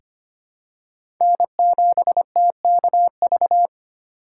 Morse code, heard as a keyed tone, about 25 wpm, 700 Hz. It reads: N7TKV
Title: USA Call Signs - 5 Characters - 3x Character Spacing - 25wpm